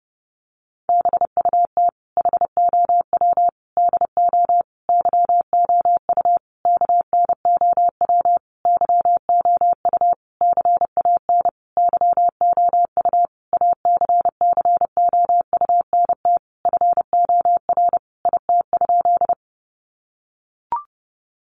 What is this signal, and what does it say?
Morse code, heard as a keyed tone, about 30 wpm, 700 Hz. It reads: BUT HOW DO YOU KNOW YOU CAN YOU ACCOUNT FOR IT?
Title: Sentences from Top 900 Words 30wpm